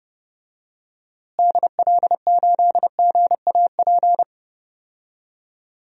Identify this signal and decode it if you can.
Morse code, heard as a keyed tone, about 30 wpm, 700 Hz. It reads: DL8GAP
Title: DXCC-100 Call Signs - 6 Characters - Repeated 2x - 30wpm